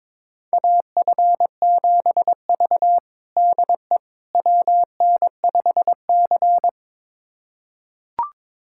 Morse code, heard as a keyed tone, about 22 wpm, 700 Hz. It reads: AF7V DE WN5C
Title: USA Callsign DE Callsign - 4 Characters - 22wpm